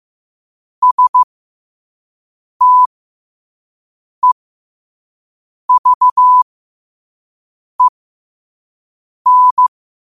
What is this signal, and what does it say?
Morse code, heard as a keyed tone, about 15 wpm, spaced as for 5 wpm, 1000 Hz. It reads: STEVEN
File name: qotdfiles / 5wpm